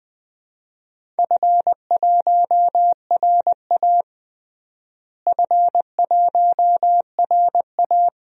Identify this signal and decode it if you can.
Morse code, heard as a keyed tone, about 21 wpm, 700 Hz. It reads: F1RA F1RA
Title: DXCC-100 Call Signs - 4 Characters - Repeated 2x - 20wpm